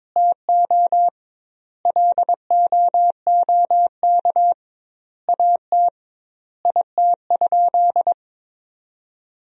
Morse code, heard as a keyed tone, about 22 wpm, 700 Hz. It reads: TO LOOK AT IT?
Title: 2x Word-Spacing - Sentences from Top 100 Words - 22wpm